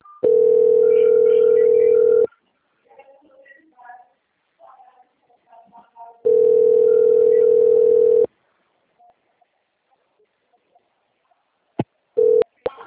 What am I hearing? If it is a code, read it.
Ringback tone (North America), 440 + 480 Hz, with speech.
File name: out- - - - - .6650